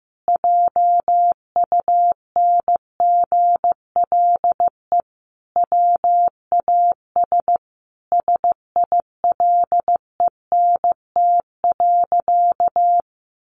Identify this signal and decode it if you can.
Morse code, heard as a keyed tone, about 15 wpm, 700 Hz. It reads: JUNGLE WAS SILENT.
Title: A Sound Of Thunder 15wpm